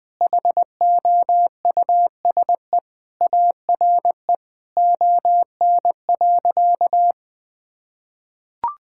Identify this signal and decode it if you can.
Morse code, heard as a keyed tone, about 20 wpm, 700 Hz. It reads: HOUSE ARE ON.
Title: Sentences with 2-Word Phrases - 20wpm